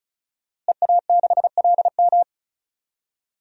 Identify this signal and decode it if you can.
Morse code, heard as a keyed tone, about 36 wpm, 700 Hz. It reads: EA6LM
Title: DXCC-100 Call Signs - 5 Characters - Repeated 3x - 35wpm